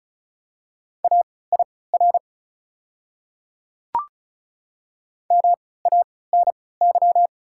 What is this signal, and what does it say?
Morse code, heard as a keyed tone, about 36 wpm, spaced as for 20 wpm, 700 Hz. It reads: AIR MANY
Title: 3x Character Spacing - Top 200 Words - 35wpm